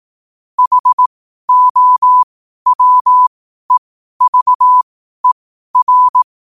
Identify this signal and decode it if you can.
Morse code, heard as a keyed tone, about 18 wpm, spaced as for 12 wpm, 1000 Hz. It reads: HOWEVER
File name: qotdfiles / 12wpm